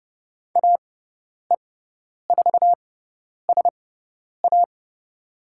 Morse code, heard as a keyed tone, about 30 wpm, spaced as for 10 wpm, 700 Hz. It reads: AE4SA
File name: Lesson 18g - Letter m - Call Signs 30wpm